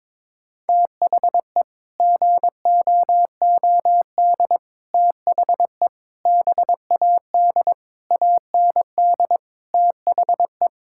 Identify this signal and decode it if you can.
Morse code, heard as a keyed tone, about 22 wpm, 700 Hz. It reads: THE GOOD THE BAD AND THE
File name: practice / 22wpm / Sets of 7 Words - 22wpm